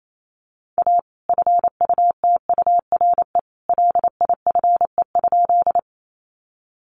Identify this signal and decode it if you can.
Morse code, heard as a keyed tone, about 28 wpm, 700 Hz. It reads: A FUTURE LIFE?
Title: Sentences from Top 700 Words 28wpm